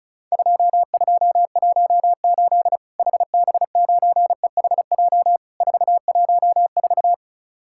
Morse code, heard as a keyed tone, about 36 wpm, 700 Hz. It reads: 2218 HB9EHJ 414
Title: Worked All Europe DX Contest - QTC Exchange - 35wpm